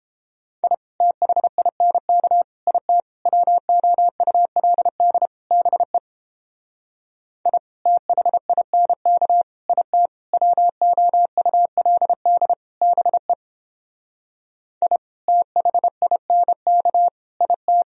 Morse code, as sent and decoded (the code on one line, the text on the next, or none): .. / - .... .. -. -.- / .. - / .-- --- ..- .-.. -.. / -... / .. / - .... .. -. -.- / .. - / .-- --- ..- .-.. -.. / -... / .. / - .... .. -. -.- / .. -
I THINK IT WOULD BE I THINK IT WOULD BE I THINK IT